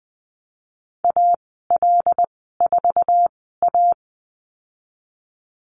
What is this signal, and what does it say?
Morse code, heard as a keyed tone, about 20 wpm, spaced as for 15 wpm, 700 Hz. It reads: AL4A
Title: 076 Lesson 13g - Letter L - Call Signs 20wpm